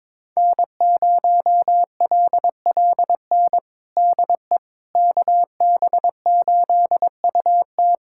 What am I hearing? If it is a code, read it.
Morse code, heard as a keyed tone, about 22 wpm, 700 Hz. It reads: N0LLN DE KB8UT
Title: USA Callsign DE Callsign - 5 Characters - 22wpm